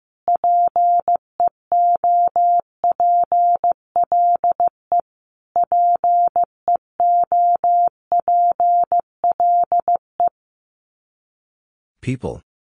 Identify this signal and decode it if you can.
Morse code, heard as a keyed tone, about 15 wpm, 700 Hz. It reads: PEOPLE PEOPLE
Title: Top 300 Words Repeated 15wpm